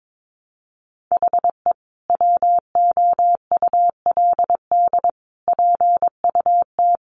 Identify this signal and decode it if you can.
Morse code, heard as a keyed tone, about 22 wpm, 700 Hz. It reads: HE WOULD PUT
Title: Easy Phrases 22wpm